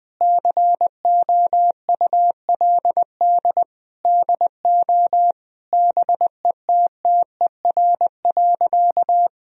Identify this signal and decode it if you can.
Morse code, heard as a keyed tone, about 20 wpm, 700 Hz. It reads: COULD DO BETTER.
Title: Sentences from Top 1000 Words - Review Part 4 - 20wpm